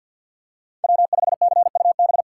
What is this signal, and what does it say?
Morse code, heard as a keyed tone, about 53 wpm, 700 Hz. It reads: W5CUB
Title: USA Call Signs - 5 Characters - 3x Character Spacing - 50wpm